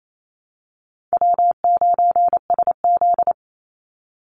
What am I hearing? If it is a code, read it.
Morse code, heard as a keyed tone, about 28 wpm, 700 Hz. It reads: W9SZ
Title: Easy Call Signs Repeated 3x - 28wpm